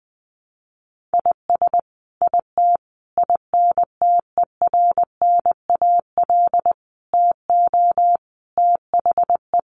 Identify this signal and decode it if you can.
Morse code, heard as a keyed tone, about 20 wpm, 700 Hz. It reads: IS IT INTERNAL TO THE
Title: Sentences from Top 1400 Words 20wpm